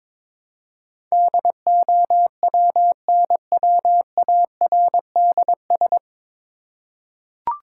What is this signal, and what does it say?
Morse code, heard as a keyed tone, about 22 wpm, 700 Hz. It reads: DOWNWARDS
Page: Morse Code Ninja practice 22wpm